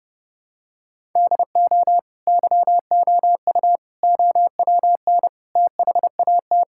Morse code, heard as a keyed tone, about 30 wpm, 700 Hz. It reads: DO YOU OWN THAT
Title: Sentences from Top 1000 Words - Review Part 6 - 30wpm